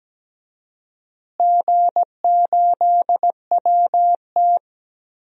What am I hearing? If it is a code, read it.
Morse code, heard as a keyed tone, about 17 wpm, 700 Hz. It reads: G8WT
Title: DXCC-100 Call Signs - 4 Characters - Repeated 2x - 17wpm